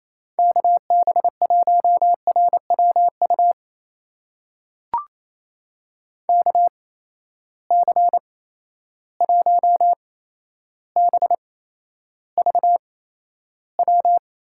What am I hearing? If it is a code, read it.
Morse code, heard as a keyed tone, about 28 wpm, spaced as for 12 wpm, 700 Hz. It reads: KB1RWU KC1BVW
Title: USA Call Signs - 6 Characters - 8x Character Spacing - 28wpm